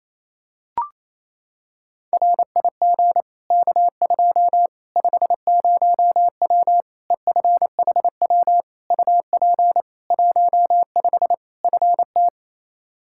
Morse code, heard as a keyed tone, about 28 wpm, 700 Hz. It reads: RIG K2 50W EFHW UP 15 FT
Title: Practice QSO - Exchange 2 - Info - 28wpm